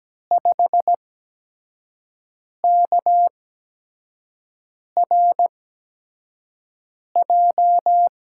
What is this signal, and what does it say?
Morse code, heard as a keyed tone, about 17 wpm, spaced as for 5 wpm, 700 Hz. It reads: 5KRJ
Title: USA Call Signs - 6 Characters - 8x Character Spacing - 17wpm